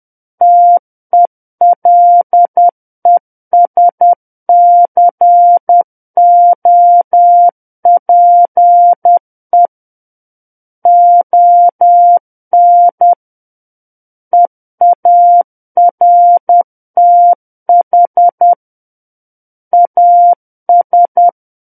Morse code, heard as a keyed tone, about 10 wpm, 700 Hz. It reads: TELESCOPE ON EARTH AS
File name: Book One / 01-Chapter 1 - 10 WPM